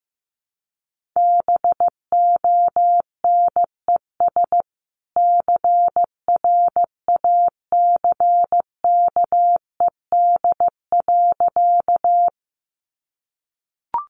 Morse code, heard as a keyed tone, about 15 wpm, 700 Hz. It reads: BONES CRACKED.
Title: A Sound Of Thunder 15wpm